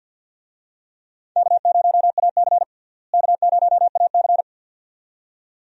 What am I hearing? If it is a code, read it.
Morse code, heard as a keyed tone, about 51 wpm, 700 Hz. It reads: K0AC K0AC
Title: USA Call Signs - 4 Characters - Repeated 2x - 50wpm